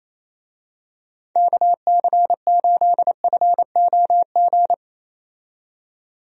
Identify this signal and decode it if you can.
Morse code, heard as a keyed tone, about 27 wpm, 700 Hz. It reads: KC8FOG
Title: USA Call Signs - 6 Characters - 28wpm